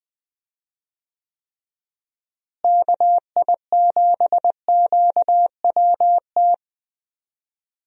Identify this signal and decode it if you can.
Morse code, heard as a keyed tone, about 20 wpm, 700 Hz. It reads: KI7QWT
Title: USA Call Signs - 6 Characters - Repeated 2x - 20wpm